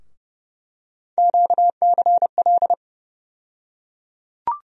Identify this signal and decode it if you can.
Morse code, heard as a keyed tone, about 30 wpm, 700 Hz. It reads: QCL
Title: USA Call Sign Postfixes - 3 Letters 30wpm